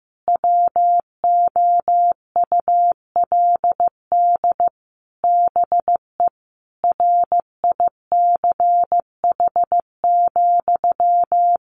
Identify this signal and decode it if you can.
Morse code, heard as a keyed tone, about 15 wpm, 700 Hz. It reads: WOULD BE RICH,
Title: Sentences from Top 1500 Words 15wpm